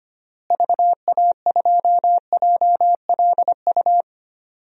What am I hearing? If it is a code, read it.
Morse code, heard as a keyed tone, about 25 wpm, 700 Hz. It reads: VA2JLU
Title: DXCC-100 Call Signs - 6 Characters - Repeated 3x - 25wpm